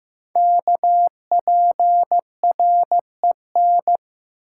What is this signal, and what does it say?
Morse code, heard as a keyed tone, about 15 wpm, 700 Hz. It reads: KPREN
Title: Code Groups - 5 Letters - 15wpm